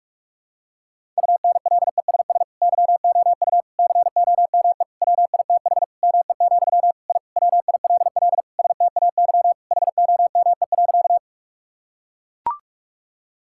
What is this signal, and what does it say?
Morse code, heard as a keyed tone, about 41 wpm, 700 Hz. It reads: UNLESS YOU COME WITH ME, I WILL STAY HOME.
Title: Sentences from Top 1000 Words - Review Part 6 - 45wpm